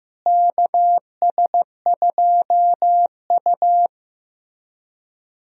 Morse code, heard as a keyed tone, about 15 wpm, 700 Hz. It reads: KS2U
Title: USA Call Signs - 4 Characters - Repeated 2x - 15wpm